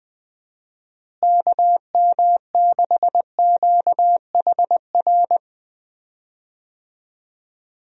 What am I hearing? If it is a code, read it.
Morse code, heard as a keyed tone, about 20 wpm, 700 Hz. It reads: KM6QHR
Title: USA Call Signs - 6 Characters - Repeated 2x - 20wpm